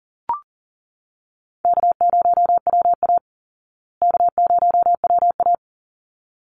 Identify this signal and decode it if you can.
Morse code, heard as a keyed tone, about 40 wpm, 700 Hz. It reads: K0WA K0WA
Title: Call Signs Repeated 3x - 40wpm